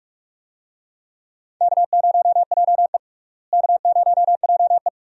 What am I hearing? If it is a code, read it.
Morse code, heard as a keyed tone, about 41 wpm, 700 Hz. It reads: K0JE K0JE
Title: USA Call Signs - 4 Characters - Repeated 2x - 45wpm